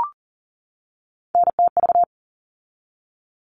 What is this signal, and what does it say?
Morse code, heard as a keyed tone, about 40 wpm, 700 Hz. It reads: NTV
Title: All Cut Numbers - 3-Digit 40wpm